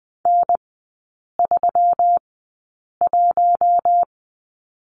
Morse code, heard as a keyed tone, about 20 wpm, spaced as for 8 wpm, 700 Hz. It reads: N31
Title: 133 Lesson 22e - Question Mark - Three Characters 20wpm